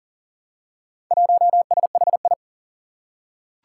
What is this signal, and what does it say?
Morse code, heard as a keyed tone, about 40 wpm, 700 Hz. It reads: 1SHI
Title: Four Challenging Letter-Number Combinations 40wpm